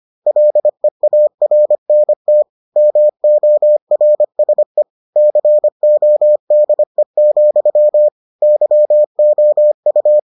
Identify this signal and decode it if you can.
Morse code, heard as a keyed tone, about 25 wpm, 600 Hz. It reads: LEARNT MORSE CODE, YOU